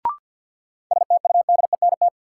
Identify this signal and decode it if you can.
Morse code, heard as a keyed tone, about 52 wpm, 700 Hz. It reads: STUDENT
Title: Sentences from Top 1000 Words - New Word Drill - 50wpm